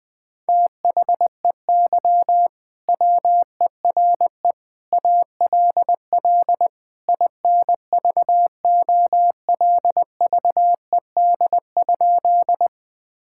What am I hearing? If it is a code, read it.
Morse code, heard as a keyed tone, about 21 wpm, 700 Hz. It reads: THEY WERE ALL INVOLVED?
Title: Sentences from Top 1000 Words - Review Part 5 - 20wpm